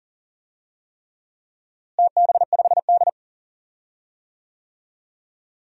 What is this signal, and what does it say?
Morse code, heard as a keyed tone, about 40 wpm, 700 Hz. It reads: TB5D
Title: DXCC-100 Call Signs - 4 Characters - Repeated 3x - 40wpm